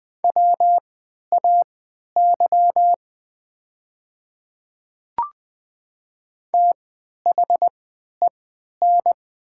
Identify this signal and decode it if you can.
Morse code, heard as a keyed tone, about 20 wpm, spaced as for 11 wpm, 690 Hz. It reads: WAY THEN
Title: 3x Character Spacing - Top 500 Words - 20wpm